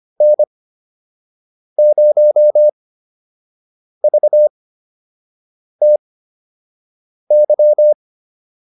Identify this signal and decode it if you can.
Morse code, heard as a keyed tone, about 25 wpm, spaced as for 6 wpm, 600 Hz. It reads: N0VTY